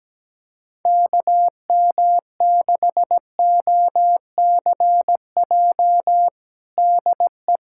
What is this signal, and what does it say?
Morse code, heard as a keyed tone, about 17 wpm, 700 Hz. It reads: KM6OCJ DE
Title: USA Callsign DE Callsign - 6 Characters - 17wpm